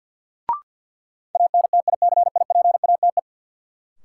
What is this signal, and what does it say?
Morse code, heard as a keyed tone, about 46 wpm, 700 Hz. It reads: ANTICIPATE
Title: Sentences from Top 1500 Words - New Word Drill Alt - 50wpm